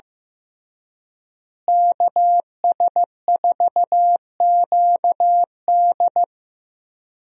Morse code, heard as a keyed tone, about 15 wpm, 700 Hz. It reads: KS4QD
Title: USA Call Signs - 5 Characters - Repeated 2x - 15wpm